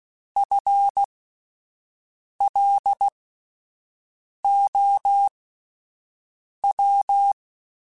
Morse code, heard as a keyed tone, about 16 wpm, spaced as for 5 wpm, 800 Hz. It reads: FLOW
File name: Alphabetical / Alphabetical mp3s / Fs @5